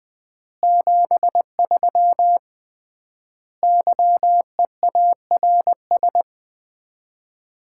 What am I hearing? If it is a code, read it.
Morse code, heard as a keyed tone, about 21 wpm, 700 Hz. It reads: 73 YEARS
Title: Common QSO Phrases - 3x Word-Spacing - 20wpm